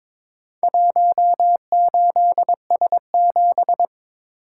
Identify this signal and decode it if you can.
Morse code, heard as a keyed tone, about 23 wpm, 700 Hz. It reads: 18S7
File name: practice / 22wpm / Four Letter-Number Combinations - 22wpm